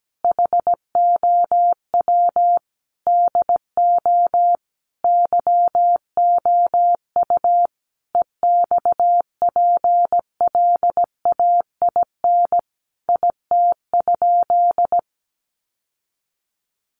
Morse code, heard as a keyed tone, about 17 wpm, 700 Hz. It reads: HOW DO YOU EXPLAIN IT?
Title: Sentences from Top 1100 Words 17wpm